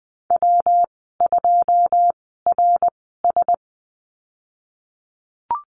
Morse code, heard as a keyed tone, about 20 wpm, spaced as for 14 wpm, 700 Hz. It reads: W2RS
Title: 128 Lesson 21g - Number 6 - Call Signs 20wpm